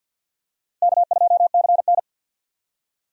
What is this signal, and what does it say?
Morse code, heard as a keyed tone, about 50 wpm, 700 Hz. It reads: K2CN
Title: USA Call Signs - 4 Characters - Repeated 2x - 50wpm